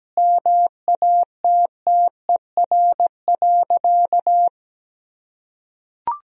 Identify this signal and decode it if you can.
Morse code, heard as a keyed tone, about 17 wpm, 700 Hz. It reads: MATTER.
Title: Sentences from Top 1000 Words - Review Part 1 - 17wpm